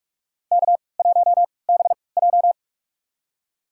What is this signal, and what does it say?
Morse code, heard as a keyed tone, about 46 wpm, spaced as for 26 wpm, 700 Hz. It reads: K1BJ
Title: USA Call Signs - 3x Character Spacing - 45wpm